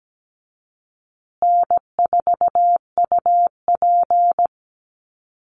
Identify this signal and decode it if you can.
Morse code, heard as a keyed tone, about 17 wpm, 700 Hz. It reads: N4UP
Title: Easy Call Signs Repeated 3x - 17wpm